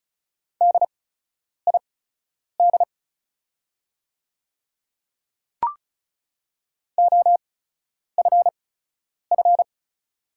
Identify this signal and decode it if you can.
Morse code, heard as a keyed tone, about 35 wpm, spaced as for 10 wpm, 690 Hz. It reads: DID OFF
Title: 8x Character Spacing - Top 200 Words - 35wpm